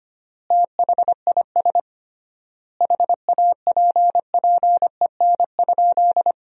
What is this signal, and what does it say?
Morse code, heard as a keyed tone, about 25 wpm, 700 Hz. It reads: THIS HAPPEN?
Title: 3x Word-Spacing - Sentences from Top 400 Words - 25wpm